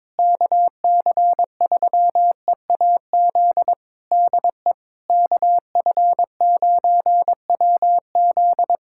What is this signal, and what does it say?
Morse code, heard as a keyed tone, about 22 wpm, 700 Hz. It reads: KC3EAZ DE KF9WZ
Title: Practice QSO - Exchange 4 - Ending - 22wpm